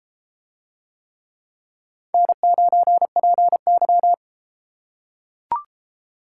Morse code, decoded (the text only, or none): N9PY